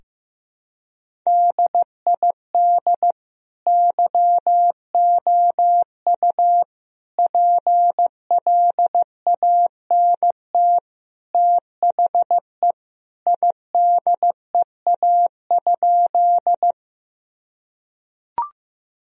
Morse code, heard as a keyed tone, about 15 wpm, 700 Hz. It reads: DID YOU PLANT THE IDEA?
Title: Sentences from Top 1000 Words - Review Part 1 - 15wpm